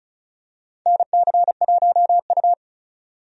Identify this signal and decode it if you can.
Morse code, heard as a keyed tone, about 36 wpm, 700 Hz. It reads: NC1U
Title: USA Call Signs - 4 Characters - 35wpm